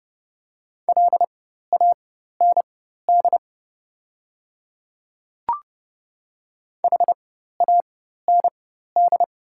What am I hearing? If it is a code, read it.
Morse code, heard as a keyed tone, about 30 wpm, spaced as for 14 wpm, 700 Hz. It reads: LAND HAND
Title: 4x Character Spacing - Top 200 Words - 30wpm